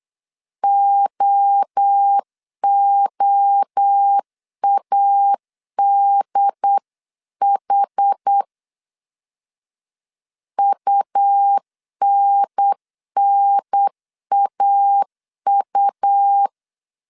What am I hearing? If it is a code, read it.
Morse code, heard as a keyed tone, about 8 wpm, 800 Hz. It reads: OOADH UNNAU